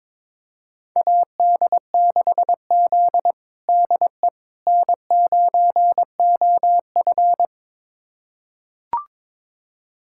Morse code, heard as a keyed tone, about 22 wpm, 700 Hz. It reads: AD6Z DE N9OF
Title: USA Callsign DE Callsign - 4 Characters - 22wpm